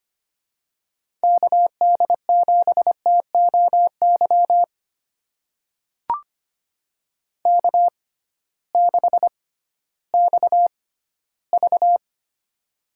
Morse code, heard as a keyed tone, about 25 wpm, spaced as for 14 wpm, 700 Hz. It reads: KD7TOY K6XV
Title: USA Call Signs - 6x Character Spacing - 25wpm